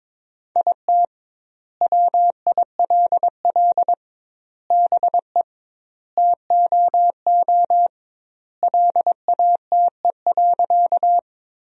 Morse code, heard as a keyed tone, about 22 wpm, 700 Hz. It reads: IT WILL BE TOO LATE.
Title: 2x Word-Spacing - Sentences from Top 300 Words - 22wpm